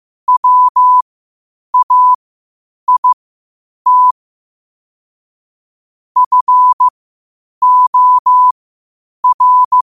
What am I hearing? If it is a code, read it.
Morse code, heard as a keyed tone, about 15 wpm, spaced as for 8 wpm, 1000 Hz. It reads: WAIT FOR
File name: qotdfiles / 8wpm